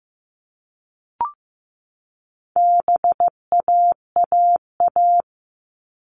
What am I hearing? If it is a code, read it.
Morse code, heard as a keyed tone, about 15 wpm, 700 Hz. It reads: BAAA
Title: All Cut Numbers - 4-Digit 15wpm